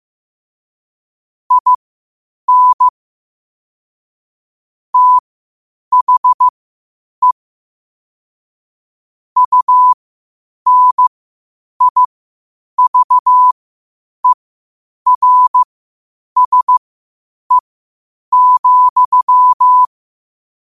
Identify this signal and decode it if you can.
Morse code, heard as a keyed tone, about 15 wpm, spaced as for 8 wpm, 1000 Hz. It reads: IN THE UNIVERSE,
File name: qotdfiles / 8wpm